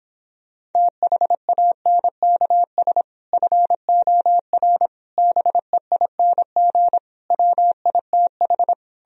Morse code, heard as a keyed tone, about 29 wpm, 700 Hz. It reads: THANKS FOR BEING WITH